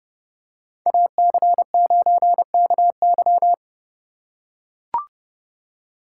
Morse code, heard as a keyed tone, about 30 wpm, 700 Hz. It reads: AC9KY